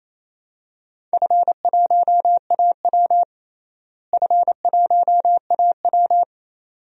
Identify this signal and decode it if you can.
Morse code, heard as a keyed tone, about 28 wpm, 700 Hz. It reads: F1AW F1AW
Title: DXCC-100 Call Signs - 4 Characters - Repeated 2x - 28wpm